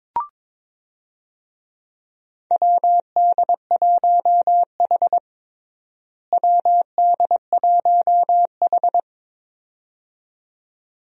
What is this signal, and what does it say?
Morse code, heard as a keyed tone, about 22 wpm, 700 Hz. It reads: WD1H WD1H
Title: USA Call Signs - 4 Characters - Repeated 2x - 22wpm